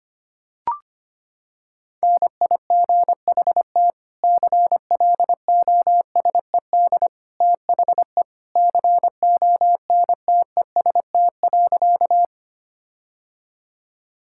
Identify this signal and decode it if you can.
Morse code, heard as a keyed tone, about 25 wpm, 700 Hz. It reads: NIGHT CLOSED THE CONTEST.
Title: Sentences from Top 1600 Words 25wpm